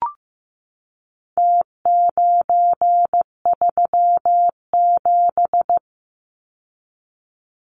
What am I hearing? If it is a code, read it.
Morse code, heard as a keyed tone, about 15 wpm, 700 Hz. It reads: T937
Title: Numbers - Contest Style - 4-Digit 15wpm